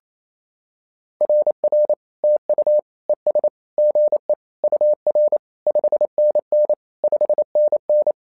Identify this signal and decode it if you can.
Morse code, heard as a keyed tone, about 27 wpm, 600 Hz. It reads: RR TU ES GE UR 5NN 5NN
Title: SOTA QSO Practice 28wpm